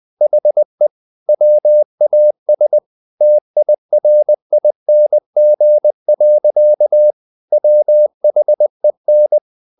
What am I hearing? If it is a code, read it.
Morse code, heard as a keyed tone, about 20 wpm, 600 Hz. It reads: HE WAS TIRING. WHEN
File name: the old man and the sea / the old man and the sea part 1 20wpm